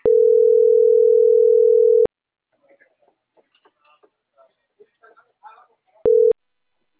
Ringback tone (North America), 440 + 480 Hz.